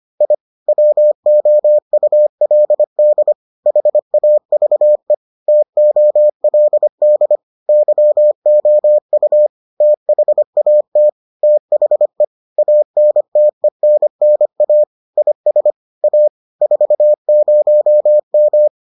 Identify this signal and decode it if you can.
Morse code, heard as a keyed tone, about 25 wpm, 600 Hz. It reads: I WOULD HAVE TOLD YOU THAT THE ANTENNA IS A 40M